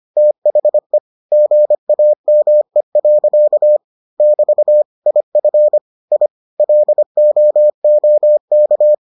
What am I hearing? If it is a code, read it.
Morse code, heard as a keyed tone, about 25 wpm, 600 Hz. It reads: THE GAME. <BT> IF I LOOK